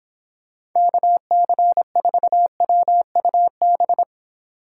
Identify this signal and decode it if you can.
Morse code, heard as a keyed tone, about 23 wpm, 700 Hz. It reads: KC4WUB